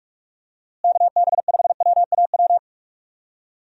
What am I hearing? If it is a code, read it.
Morse code, heard as a keyed tone, about 46 wpm, 700 Hz. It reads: KB5WAW
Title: USA Call Signs - 6 Characters - Repeated 3x - 45wpm